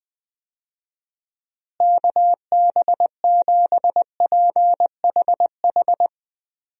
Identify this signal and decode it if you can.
Morse code, heard as a keyed tone, about 20 wpm, 700 Hz. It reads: KB7PHH
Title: USA Call Signs - 6 Characters - Repeated 3x - 20wpm